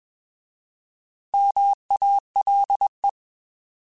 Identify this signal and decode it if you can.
Morse code, heard as a keyed tone, about 21 wpm, 800 Hz. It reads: MALE
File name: Alphabetical / Alphabetical mp3s / Ms @20